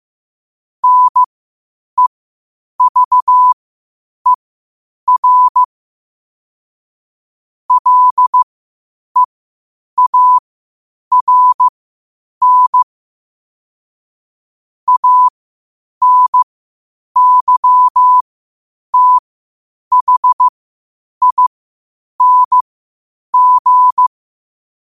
Morse code, heard as a keyed tone, about 15 wpm, spaced as for 8 wpm, 1000 Hz. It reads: NEVER LEARN ANYTHING
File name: qotdfiles / 8wpm